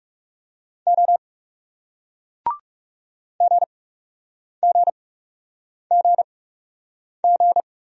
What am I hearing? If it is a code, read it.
Morse code, heard as a keyed tone, about 39 wpm, spaced as for 8 wpm, 700 Hz. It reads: O GGGG